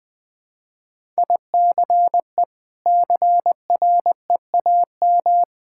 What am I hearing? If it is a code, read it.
Morse code, heard as a keyed tone, about 20 wpm, 700 Hz. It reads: ICE CREAM
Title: Easy Sentences with Hints - 20wpm